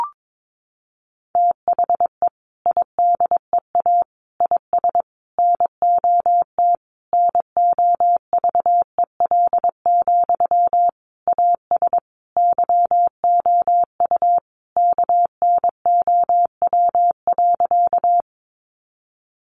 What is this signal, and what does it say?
Morse code, heard as a keyed tone, about 22 wpm, 700 Hz. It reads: THE IDEA IS NOT NOVEL, AS YOU KNOW.
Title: Sentences from Top 1300 Words 22wpm